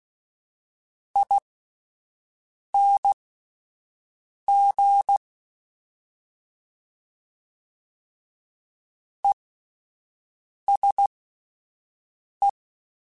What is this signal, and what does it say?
Morse code, heard as a keyed tone, about 16 wpm, spaced as for 5 wpm, 800 Hz. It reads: ING ESE